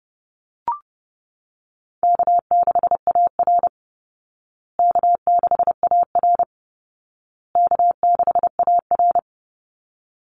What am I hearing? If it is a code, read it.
Morse code, heard as a keyed tone, about 30 wpm, 700 Hz. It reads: K6AR K6AR K6AR
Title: Easy Call Signs Repeated 3x - 30wpm